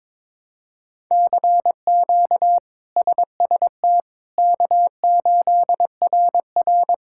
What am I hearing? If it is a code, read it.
Morse code, heard as a keyed tone, about 22 wpm, 700 Hz. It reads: CQ SST K8RR
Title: K1USN SST QSO Practice - 22wpm